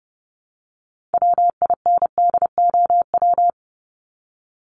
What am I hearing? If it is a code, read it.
Morse code, heard as a keyed tone, about 30 wpm, 700 Hz. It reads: WINDOW
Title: Top 1501-1750 Words 30wpm